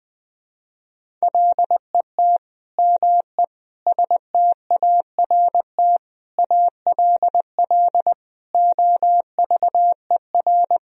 Morse code, heard as a keyed tone, about 20 wpm, 700 Hz. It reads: LET ME START ALL OVER
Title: Sentences from Top 1000 Words - Review Part 1 - 20wpm